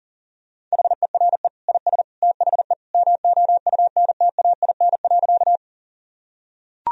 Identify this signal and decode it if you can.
Morse code, heard as a keyed tone, about 37 wpm, 700 Hz. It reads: HERE IS THE MOUNTAIN.
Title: Sentences from Top 1000 Words - Review Part 1 - 40wpm